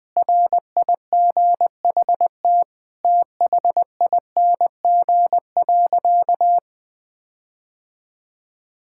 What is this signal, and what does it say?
Morse code, heard as a keyed tone, about 20 wpm, 700 Hz. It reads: RIGHT THING.
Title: Sentences with 5-Word Phrases - 20wpm